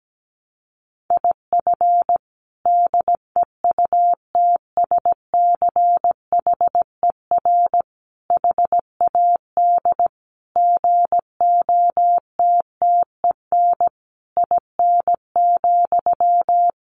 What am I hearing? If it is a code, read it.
Morse code, heard as a keyed tone, about 17 wpm, 700 Hz. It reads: IF DEUTSCHER HAD GOTTEN IN,
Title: A Sound Of Thunder 17wpm